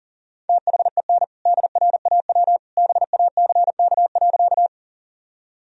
Morse code, heard as a keyed tone, about 40 wpm, 700 Hz. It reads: THEN DRAW BACK.
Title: Sentences from Top 1000 Words - Review Part 1 - 40wpm